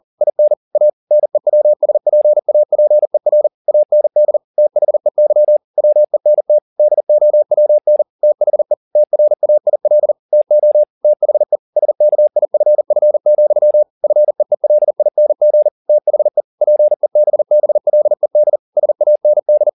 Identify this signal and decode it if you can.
Morse code, heard as a keyed tone, about 40 wpm, 600 Hz. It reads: IN A NEWSPAPER AND THEY WENT DOWN THE TRAIL TO THE SKIFF, FEELING THE PEBBLED SAND